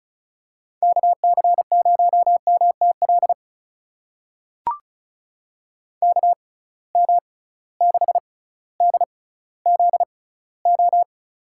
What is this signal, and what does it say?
Morse code, heard as a keyed tone, about 36 wpm, spaced as for 18 wpm, 700 Hz. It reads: KC0MTL KM6DZO
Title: USA Call Signs - 6 Characters - 6x Character Spacing - 35wpm